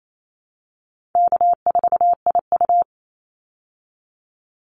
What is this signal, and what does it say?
Morse code, heard as a keyed tone, about 28 wpm, 700 Hz. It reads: K4IU
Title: Call Signs - 28wpm